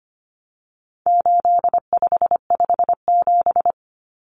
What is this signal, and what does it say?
Morse code, heard as a keyed tone, about 25 wpm, 700 Hz. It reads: 8557
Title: Numbers - 4-Digit 25wpm